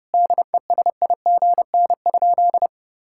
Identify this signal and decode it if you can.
Morse code, heard as a keyed tone, about 30 wpm, 700 Hz. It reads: DESIGN?
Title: Sentences from Top 1000 Words - Review Part 5 - 30wpm